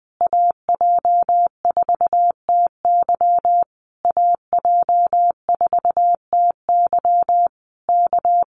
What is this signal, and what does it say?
Morse code, heard as a keyed tone, about 19 wpm, 700 Hz. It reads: AJ4TY AJ4TY K
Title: Common QSO Phrases Encore 20wpm